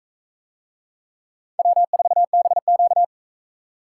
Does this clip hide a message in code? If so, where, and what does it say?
Yes: Morse code from 1.6 to 3.0 s, heard as a keyed tone, about 38 wpm, 700 Hz. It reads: W4BQ